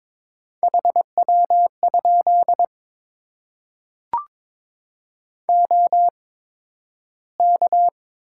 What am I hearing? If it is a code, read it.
Morse code, heard as a keyed tone, about 23 wpm, spaced as for 12 wpm, 700 Hz. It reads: HW? OK
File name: Top 90 QSO Elements - 8x Character Spacing - 22wpm